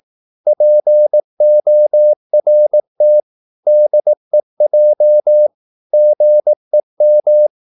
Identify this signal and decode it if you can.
Morse code, heard as a keyed tone, about 18 wpm, 600 Hz. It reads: PORT DEJ GEM
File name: Korte ord DK 18wpm